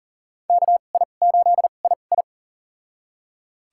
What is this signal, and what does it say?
Morse code, heard as a keyed tone, about 40 wpm, spaced as for 29 wpm, 700 Hz. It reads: KI8II
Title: USA Call Signs - 5 Characters - 2x Character Spacing - 40wpm